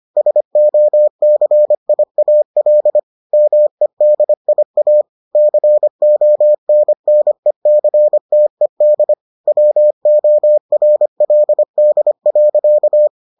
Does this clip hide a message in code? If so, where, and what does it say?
Yes: Morse code from 0.2 to 13.1 s, heard as a keyed tone, about 25 wpm, 600 Hz. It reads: SOCIAL MEDIA CONNECTED WORLD.